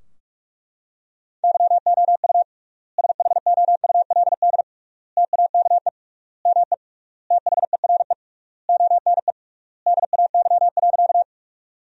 Morse code, heard as a keyed tone, about 41 wpm, 700 Hz. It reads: YOU SHOULD TAKE ME THERE ONE DAY.
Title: 3x Word-Spacing - Sentences from Top 300 Words - 45wpm